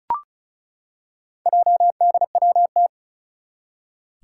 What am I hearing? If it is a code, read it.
Morse code, heard as a keyed tone, about 36 wpm, 700 Hz. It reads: JDWT
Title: Four Challenging Letter-Number Combinations 35wpm